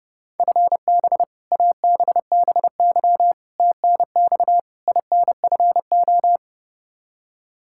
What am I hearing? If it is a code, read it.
Morse code, heard as a keyed tone, about 32 wpm, 700 Hz. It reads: FB ABBY TNX INFO
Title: Practice QSO - Exchange 3 - More Info - 30wpm